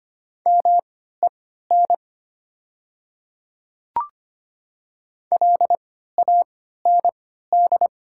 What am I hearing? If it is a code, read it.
Morse code, heard as a keyed tone, about 25 wpm, spaced as for 14 wpm, 700 Hz. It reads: MEN LAND
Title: 3x Character Spacing - Top 200 Words - 25wpm